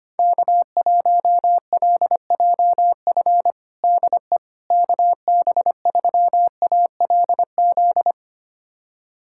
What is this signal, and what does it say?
Morse code, heard as a keyed tone, about 25 wpm, 700 Hz. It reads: K1LJF DE KB3ALZ
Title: USA Callsign DE Callsign 25wpm